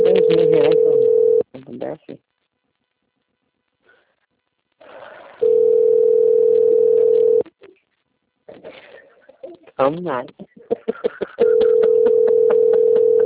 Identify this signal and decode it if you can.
Ringback tone (North America), 440 + 480 Hz, with speech.
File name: rg-1600-unknown- - - .84345